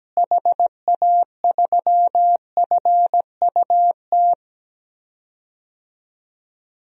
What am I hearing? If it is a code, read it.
Morse code, heard as a keyed tone, about 17 wpm, 700 Hz. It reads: HA3FUT
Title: DXCC-100 Call Signs - 6 Characters - Repeated 2x - 17wpm